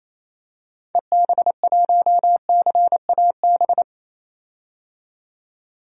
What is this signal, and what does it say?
Morse code, heard as a keyed tone, about 28 wpm, 700 Hz. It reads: EB1CAB
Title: DXCC-100 Call Signs - 6 Characters - 28wpm